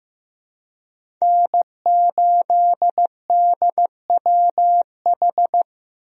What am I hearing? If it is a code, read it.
Morse code, heard as a keyed tone, about 15 wpm, 700 Hz. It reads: N8DWH
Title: USA Call Signs - 5 Characters - 15wpm